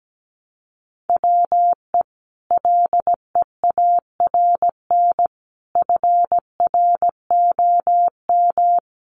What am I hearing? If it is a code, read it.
Morse code, heard as a keyed tone, about 17 wpm, 700 Hz. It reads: WE LEARN FROM
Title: Sentences from Top 900 Words 17wpm